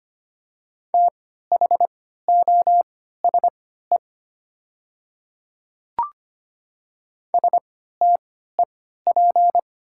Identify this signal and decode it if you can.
Morse code, heard as a keyed tone, about 25 wpm, spaced as for 14 wpm, 700 Hz. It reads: THOSE STEP
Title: 3x Character Spacing - Top 500 Words - 25wpm